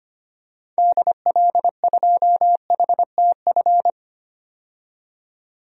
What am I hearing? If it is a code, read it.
Morse code, heard as a keyed tone, about 25 wpm, 700 Hz. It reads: DL2HTF
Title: DXCC-100 Call Signs - 6 Characters - 25wpm